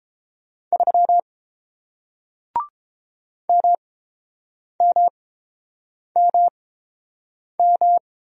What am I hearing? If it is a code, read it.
Morse code, heard as a keyed tone, about 29 wpm, spaced as for 7 wpm, 700 Hz. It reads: 3 MMMM